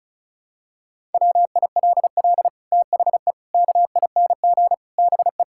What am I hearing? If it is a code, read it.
Morse code, heard as a keyed tone, about 36 wpm, 700 Hz. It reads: WILL THE KING BE
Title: Sentences from Top 1000 Words - Review Part 6 - 35wpm